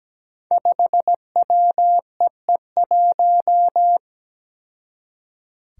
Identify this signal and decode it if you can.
Morse code, heard as a keyed tone, about 17 wpm, 700 Hz. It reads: 5WEE1
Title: Five Challenging Letter-Number Combinations 17wpm